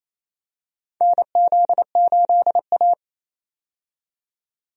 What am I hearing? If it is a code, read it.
Morse code, heard as a keyed tone, about 28 wpm, 700 Hz. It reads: NZ8A
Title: USA Call Signs - 4 Characters - 28wpm